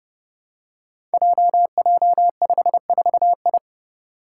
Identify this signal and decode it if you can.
Morse code, heard as a keyed tone, about 30 wpm, 700 Hz. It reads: JJ54I
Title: Five Challenging Letter-Number Combinations 30wpm